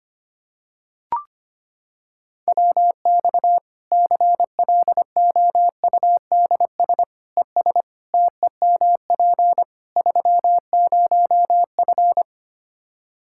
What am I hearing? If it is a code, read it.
Morse code, heard as a keyed tone, about 25 wpm, 700 Hz. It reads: WX CLOUDS ES TEMP 30F
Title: K1USN SST QSO Practice 25wpm